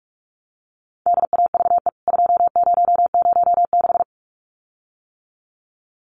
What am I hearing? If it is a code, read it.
Morse code, heard as a keyed tone, about 44 wpm, 700 Hz. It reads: DAVE 2006
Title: CWT Exchange 45wpm